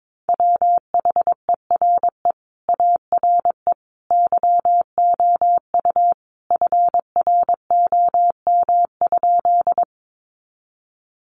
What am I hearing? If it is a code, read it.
Morse code, heard as a keyed tone, about 22 wpm, 700 Hz. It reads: WHERE ARE YOU FROM?
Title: Sentences from Top 200 Words 22wpm